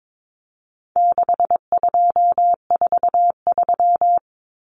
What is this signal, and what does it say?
Morse code, heard as a keyed tone, about 22 wpm, 700 Hz. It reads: 6243
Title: Numbers - 4-Digit 22wpm